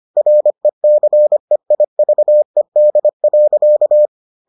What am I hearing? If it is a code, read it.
Morse code, heard as a keyed tone, about 25 wpm, 600 Hz. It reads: RECEIVED.